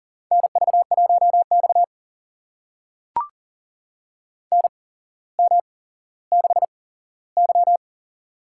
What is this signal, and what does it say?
Morse code, heard as a keyed tone, about 37 wpm, spaced as for 17 wpm, 700 Hz. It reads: NV1X NM6Y
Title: USA Call Signs - 4 Characters - 8x Character Spacing - 40wpm